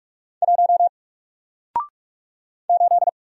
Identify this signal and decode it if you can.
Morse code, heard as a keyed tone, about 46 wpm, 700 Hz. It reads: 1 8
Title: Single Character (Letter:Number:Punctuation:Prosign) Alt 45wpm